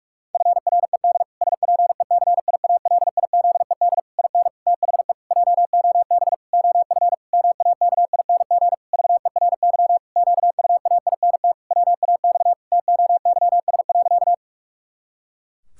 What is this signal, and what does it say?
Morse code, heard as a keyed tone, about 46 wpm, 700 Hz. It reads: FRED SPECIALIZED IN THE JOB OF MAKING VERY QUAINT WAX TOYS.
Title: Pangram Challenge v2 - 45wpm